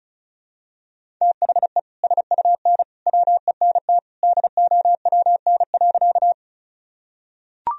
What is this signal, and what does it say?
Morse code, heard as a keyed tone, about 36 wpm, 700 Hz. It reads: THE SUN WENT DOWN.
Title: Sentences from Top 1000 Words - Review Part 1 - 35wpm